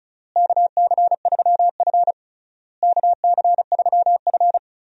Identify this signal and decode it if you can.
Morse code, heard as a keyed tone, about 36 wpm, 700 Hz. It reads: KC3F KC3F
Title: USA Call Signs - 4 Characters - Repeated 3x - 35wpm